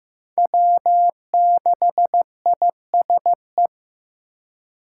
Morse code, heard as a keyed tone, about 15 wpm, 700 Hz. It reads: W6ISE
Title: USA Call Signs - 6x Character Spacing - 15wpm